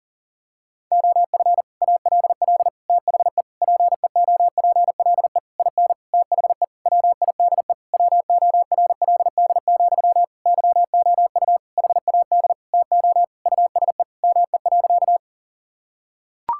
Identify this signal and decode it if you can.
Morse code, heard as a keyed tone, about 40 wpm, 700 Hz. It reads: OF ALL THE PEOPLE IN THE WIDE WORLD, YOU HAD TO USE ME.
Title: Sentences from Top 1000 Words - Review Part 6 - 40wpm